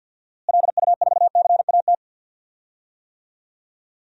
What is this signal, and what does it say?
Morse code, heard as a keyed tone, about 50 wpm, 700 Hz. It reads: LU4CAT
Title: DXCC-100 Call Signs - 6 Characters - Repeated 2x - 50wpm